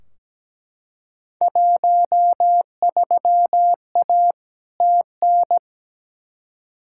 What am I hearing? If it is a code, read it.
Morse code, heard as a keyed tone, about 17 wpm, 700 Hz. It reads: 13A TN
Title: Field Day Exchanges Encore - 17wpm